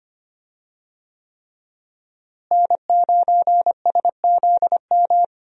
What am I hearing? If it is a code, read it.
Morse code, heard as a keyed tone, about 25 wpm, 700 Hz. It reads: N9SZM
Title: USA Call Signs - 5 Characters - Repeated 3x - 25wpm